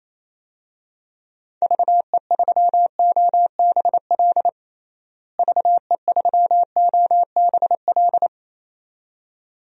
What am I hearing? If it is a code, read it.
Morse code, heard as a keyed tone, about 28 wpm, 700 Hz. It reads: VE3OBL VE3OBL
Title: DXCC-100 Call Signs - 6 Characters - Repeated 2x - 28wpm